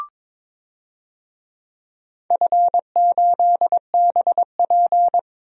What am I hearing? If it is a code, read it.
Morse code, heard as a keyed tone, about 21 wpm, 700 Hz. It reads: F8BP
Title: DXCC-100 Call Signs - 4 Characters - 22wpm